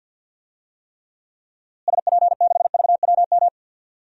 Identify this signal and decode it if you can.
Morse code, heard as a keyed tone, about 47 wpm, 700 Hz. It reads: SP6VWM
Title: DXCC-100 Call Signs - 6 Characters - Repeated 2x - 50wpm